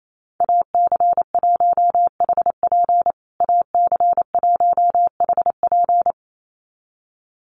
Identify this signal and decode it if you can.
Morse code, heard as a keyed tone, about 28 wpm, 700 Hz. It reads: AC1HP AC1HP
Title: USA Call Signs Repeated 2x 28wpm